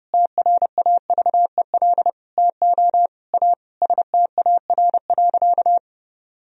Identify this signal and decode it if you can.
Morse code, heard as a keyed tone, about 30 wpm, 700 Hz. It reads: TRAVEL TO A STAR.
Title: Sentences from Top 1000 Words - Review Part 2 - 30wpm